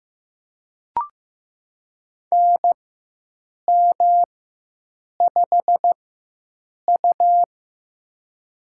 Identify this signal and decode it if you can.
Morse code, heard as a keyed tone, about 15 wpm, spaced as for 7 wpm, 700 Hz. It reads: NM5U
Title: USA Call Signs - 4 Characters - 4x Character Spacing - 15wpm